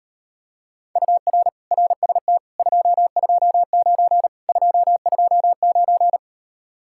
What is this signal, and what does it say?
Morse code, heard as a keyed tone, about 32 wpm, 700 Hz. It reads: UR RST 229 229